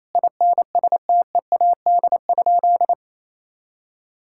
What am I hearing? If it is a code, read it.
Morse code, heard as a keyed tone, about 27 wpm, 700 Hz. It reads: INSTEAD?
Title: Sentences from Top 1000 Words - Review Part 6 - 28wpm